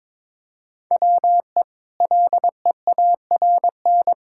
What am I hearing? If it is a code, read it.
Morse code, heard as a keyed tone, about 22 wpm, 700 Hz. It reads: WE LEARN
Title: Sentences from Top 1000 Words - Review Part 5 - 22wpm